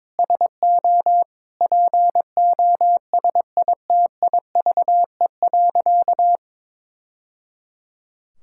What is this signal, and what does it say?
Morse code, heard as a keyed tone, about 22 wpm, 700 Hz. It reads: SO POSITIVE.
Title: Sentences from Top 1000 Words - Review Part 5 - 22wpm